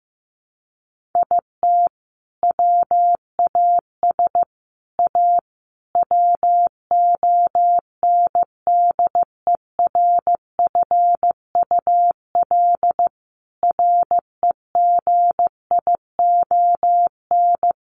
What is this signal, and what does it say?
Morse code, heard as a keyed tone, about 15 wpm, 700 Hz. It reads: IT WAS A WONDERFUL REGION
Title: Sentences from Top 1500 Words 15wpm